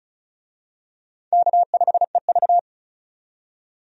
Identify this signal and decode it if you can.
Morse code, heard as a keyed tone, about 35 wpm, 700 Hz. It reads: K5EV
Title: USA Call Signs - 4 Characters - 35wpm